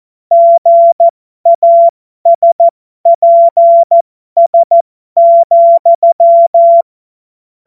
Morse code, heard as a keyed tone, about 14 wpm, spaced as for 12 wpm, 680 Hz. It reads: GASPS,
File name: Morsekurs / Witze